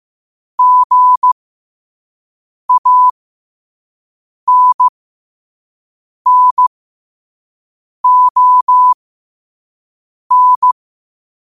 Morse code, heard as a keyed tone, about 15 wpm, spaced as for 5 wpm, 1000 Hz. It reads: GANNON
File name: qotdfiles / 5wpm